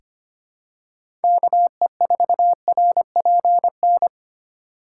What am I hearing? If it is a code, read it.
Morse code, heard as a keyed tone, about 25 wpm, 700 Hz. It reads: KE4RPN
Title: USA Call Signs - 3x Character Spacing - 25wpm